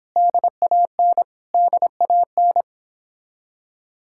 Morse code, heard as a keyed tone, about 30 wpm, 700 Hz. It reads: DAN DAN